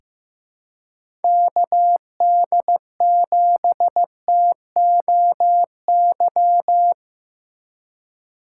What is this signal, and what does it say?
Morse code, heard as a keyed tone, about 15 wpm, 700 Hz. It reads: KD7TOY
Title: USA Call Signs - 6 Characters - 4x Character Spacing - 15wpm